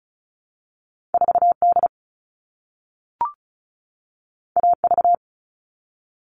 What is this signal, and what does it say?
Morse code, heard as a keyed tone, about 36 wpm, 700 Hz. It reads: 4D AV
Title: All Cut Numbers - 2-Digit 35wpm